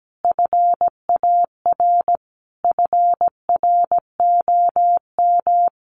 Morse code, heard as a keyed tone, about 17 wpm, 700 Hz. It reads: FAR FROM
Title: Sentences from Top 1400 Words 17wpm